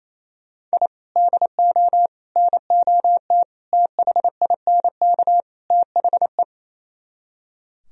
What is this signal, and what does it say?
Morse code, heard as a keyed tone, about 34 wpm, 700 Hz. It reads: I DO NOT THINK THE
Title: Speed-Racing - Sets of 5 Words 28wpm